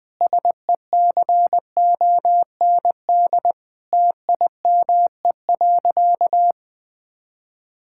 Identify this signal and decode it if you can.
Morse code, heard as a keyed tone, about 20 wpm, 700 Hz. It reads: SECOND TIME.
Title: Sentences from Top 1000 Words - Review Part 5 - 20wpm